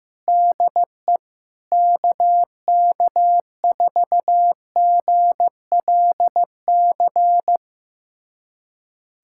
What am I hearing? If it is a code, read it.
Morse code, heard as a keyed tone, about 15 wpm, 700 Hz. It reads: DE KK4GLC
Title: USA Callsign DE Callsign 15wpm